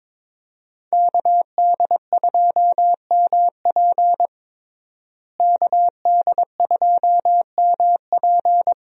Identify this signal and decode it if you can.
Morse code, heard as a keyed tone, about 22 wpm, 700 Hz. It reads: KD2MP KD2MP
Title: USA Call Signs - 5 Characters - Repeated 2x - 22wpm